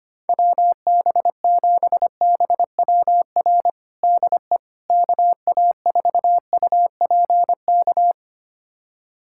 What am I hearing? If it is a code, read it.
Morse code, heard as a keyed tone, about 25 wpm, 700 Hz. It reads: WB7BWR DE KA4UPK
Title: USA Callsign DE Callsign 25wpm